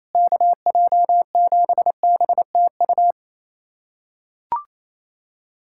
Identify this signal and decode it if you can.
Morse code, heard as a keyed tone, about 28 wpm, 700 Hz. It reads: KJ7BTU
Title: USA Callsign DE Callsign - 6 Characters - 28wpm